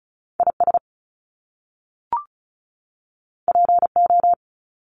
Morse code, heard as a keyed tone, about 34 wpm, 700 Hz. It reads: I S P O
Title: Top Two Letter Combinations in Words 35wpm